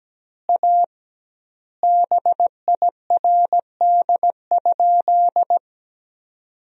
Morse code, heard as a keyed tone, about 17 wpm, 700 Hz. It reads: A BIRD?
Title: 2x Word-Spacing - Sentences from Top 400 Words - 17wpm